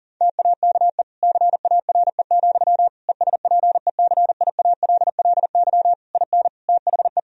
Morse code, heard as a keyed tone, about 40 wpm, 700 Hz. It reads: TAKE CARE, ESPECIALLY IN THE
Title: Sentences from Top 1000 Words - Review Part 4 - 40wpm